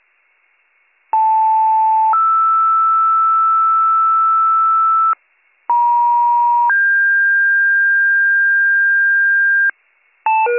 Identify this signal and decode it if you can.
Two-tone page, heard as a one-second tone, then a three-second tone, about 880 then 1320 Hz, about 950 then 1600 Hz, with noise. It is heard from 1.1 to 9.7 s.